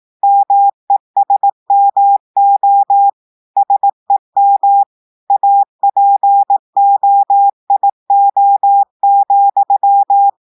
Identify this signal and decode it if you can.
Morse code, heard as a keyed tone, about 18 wpm, 800 Hz. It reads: MESMO SEM APOIO,
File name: audios / portugais / portugais texte en clair 1 vitesse 18 mots minutes